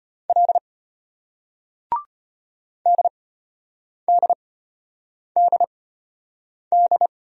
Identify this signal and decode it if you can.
Morse code, heard as a keyed tone, about 33 wpm, spaced as for 8 wpm, 700 Hz. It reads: L DDDD